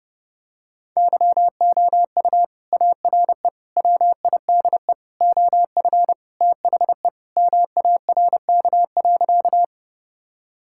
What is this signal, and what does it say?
Morse code, heard as a keyed tone, about 32 wpm, 700 Hz. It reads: YOU ARE WIDE OF THE MARK.
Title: Sentences from Top 1000 Words - Review Part 6 - 30wpm